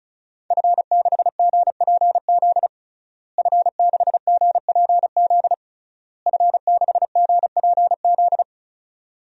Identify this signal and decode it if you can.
Morse code, heard as a keyed tone, about 36 wpm, 700 Hz. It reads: F6GPZ F6GPZ F6GPZ
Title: DXCC-100 Call Signs - 5 Characters - Repeated 3x - 35wpm